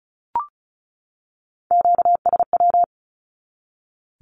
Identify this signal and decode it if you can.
Morse code, heard as a keyed tone, about 34 wpm, 700 Hz. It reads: QSW
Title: Q-Codes 35wpm